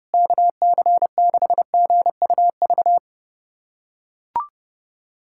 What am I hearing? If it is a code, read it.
Morse code, heard as a keyed tone, about 30 wpm, 700 Hz. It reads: KC6GUV
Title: USA Callsign DE Callsign 30wpm